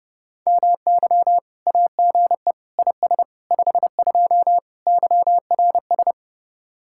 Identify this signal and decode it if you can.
Morse code, heard as a keyed tone, about 37 wpm, 700 Hz. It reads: MY AGE IS 52 YRS
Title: Speed-Racing Common QSO Phrases Encore - 30wpm